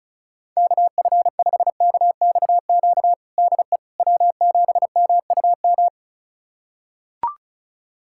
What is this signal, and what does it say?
Morse code, heard as a keyed tone, about 36 wpm, 700 Hz. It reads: KF5KXQ DE W7MUM
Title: USA Callsign DE Callsign 35wpm